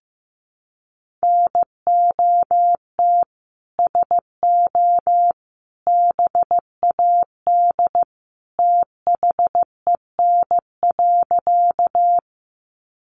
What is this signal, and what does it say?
Morse code, heard as a keyed tone, about 15 wpm, 700 Hz. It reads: NOT SO BAD THEN.
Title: Sentences from Top 700 Words 15wpm